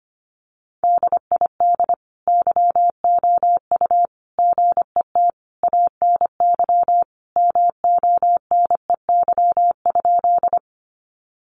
Morse code, heard as a keyed tone, about 25 wpm, 700 Hz. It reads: DID YOU GET ANY MONEY?
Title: Sentences from Top 400 Words 25wpm